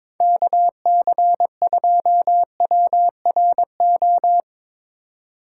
Morse code, heard as a keyed tone, about 24 wpm, 700 Hz. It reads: KC2WRO
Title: Speed Racing - USA 2x3 Call Signs 22wpm